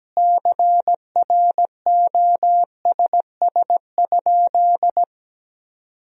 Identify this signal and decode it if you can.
Morse code, heard as a keyed tone, about 17 wpm, 700 Hz. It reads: CROSS?
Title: 2x Word-Spacing - Sentences from Top 300 Words - 17wpm